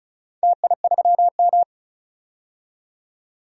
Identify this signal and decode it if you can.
Morse code, heard as a keyed tone, about 36 wpm, 700 Hz. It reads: TI3M
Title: DXCC-100 Call Signs - 4 Characters - 35wpm